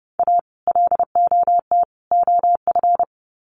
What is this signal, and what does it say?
Morse code, heard as a keyed tone, about 28 wpm, 700 Hz. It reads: A LOT OF
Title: Sets of 3 Words 30wpm